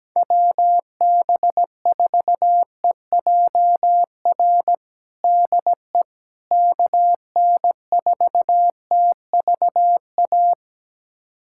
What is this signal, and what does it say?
Morse code, heard as a keyed tone, about 17 wpm, 700 Hz. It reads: WB4EJR DE KN4TVA
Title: USA Callsign DE Callsign 17wpm